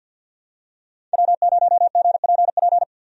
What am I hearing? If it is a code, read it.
Morse code, heard as a keyed tone, about 51 wpm, 700 Hz. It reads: W0GPP
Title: USA Call Signs - 5 Characters - 50wpm